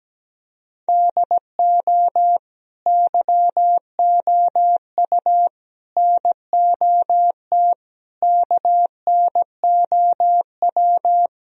Morse code, heard as a keyed tone, about 17 wpm, 700 Hz. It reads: DO YOU NOT KNOW
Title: Sentences from Top 1000 Words - Review Part 3 - 17wpm